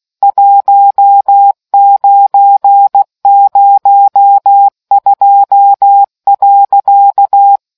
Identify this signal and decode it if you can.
Morse code, heard as a keyed tone, about 16 wpm, 800 Hz. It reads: 1902.